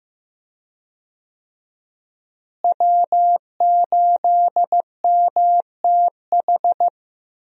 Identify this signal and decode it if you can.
Morse code, heard as a keyed tone, about 15 wpm, 700 Hz. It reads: W8MTH
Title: USA Call Signs - 5 Characters - Repeated 2x - 15wpm